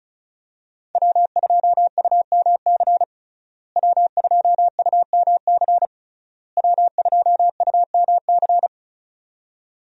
Morse code, heard as a keyed tone, about 36 wpm, 700 Hz. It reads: W2UMC W2UMC W2UMC
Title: USA Call Signs - 5 Characters - Repeated 3x - 35wpm